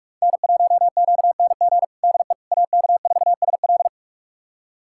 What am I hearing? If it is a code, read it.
Morse code, heard as a keyed tone, about 46 wpm, 700 Hz. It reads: N1QNG DE AK4SL
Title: USA Callsign DE Callsign - 5 Characters - 45wpm